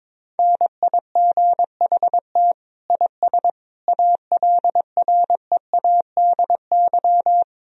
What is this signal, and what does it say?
Morse code, heard as a keyed tone, about 21 wpm, 700 Hz. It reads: NIGHT IS ALREADY
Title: Sentences from Top 1000 Words - Review Part 1 - 22wpm